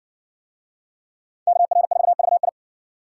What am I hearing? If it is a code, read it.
Morse code, heard as a keyed tone, about 59 wpm, 700 Hz. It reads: KR4VI